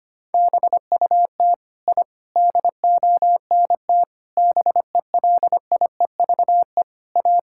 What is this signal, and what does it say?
Morse code, heard as a keyed tone, about 25 wpm, 700 Hz. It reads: BUT I DONT BELIEVE A
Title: Sentences from Top 1000 Words - Review Part 6 - 25wpm